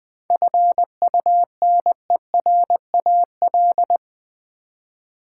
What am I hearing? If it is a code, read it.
Morse code, heard as a keyed tone, about 20 wpm, 700 Hz. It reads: FUNERAL
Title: 7 Letter Words - First 3 Letters Given - 20wpm